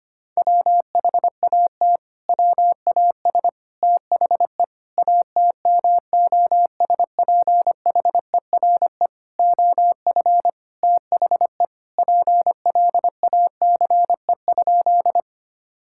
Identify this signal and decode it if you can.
Morse code, heard as a keyed tone, about 25 wpm, 700 Hz. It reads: WHAT WAS THE ATMOSPHERE OF THE PLACE?
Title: Sentences from Top 1600 Words 25wpm